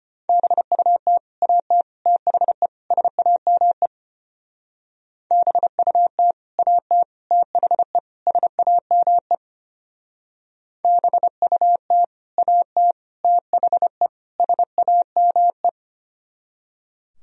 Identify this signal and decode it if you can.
Morse code, heard as a keyed tone, about 32 wpm, 700 Hz. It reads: BUT AT THE SAME BUT AT THE SAME BUT AT THE SAME